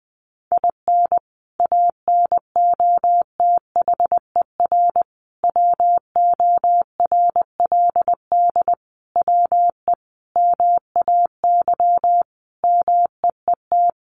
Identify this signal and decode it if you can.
Morse code, heard as a keyed tone, about 20 wpm, 700 Hz. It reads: IN ANOTHER WORLD WE MAY MEET
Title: Sentences from Top 1100 Words 20wpm